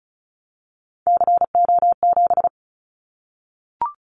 Morse code, heard as a keyed tone, about 34 wpm, 700 Hz. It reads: CO7
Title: Callsign Prefixes for Top 100 DXCC - 35wpm